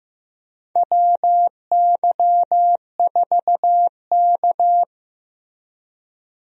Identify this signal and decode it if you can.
Morse code, heard as a keyed tone, about 15 wpm, 700 Hz. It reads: WY4K
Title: USA Call Signs - 4 Characters - Repeated 2x - 15wpm